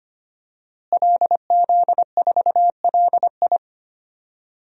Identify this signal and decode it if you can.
Morse code, heard as a keyed tone, about 25 wpm, 700 Hz. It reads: LZ4LI
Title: DXCC-100 Call Signs - 5 Characters - Repeated 3x - 25wpm